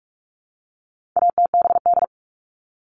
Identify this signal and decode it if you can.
Morse code, heard as a keyed tone, about 44 wpm, 700 Hz. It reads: ATBD
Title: All Cut Numbers - 4-Digit 45wpm